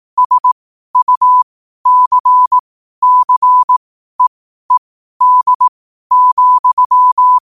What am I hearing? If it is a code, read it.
Morse code, heard as a keyed tone, about 18 wpm, spaced as for 12 wpm, 1000 Hz. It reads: SUCCEED,
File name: qotdfiles / 12wpm